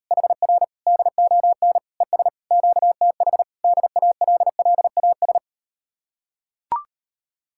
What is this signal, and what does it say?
Morse code, heard as a keyed tone, about 38 wpm, 700 Hz. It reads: HR DON ES QTH DALLAS